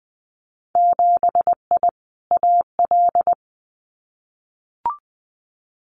Morse code, heard as a keyed tone, about 20 wpm, 700 Hz. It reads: 7I AL
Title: Winter Field Day Exchanges Encore 20wpm